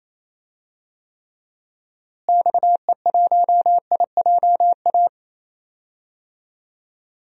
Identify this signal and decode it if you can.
Morse code, heard as a keyed tone, about 28 wpm, 700 Hz. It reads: XE1IJA
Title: DXCC-100 Call Signs - 6 Characters - Repeated 2x - 28wpm